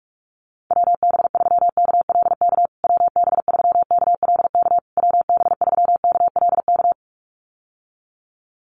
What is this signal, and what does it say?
Morse code, heard as a keyed tone, about 46 wpm, 700 Hz. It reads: WB3KLK WB3KLK WB3KLK
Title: USA Call Signs Repeated 3x 45wpm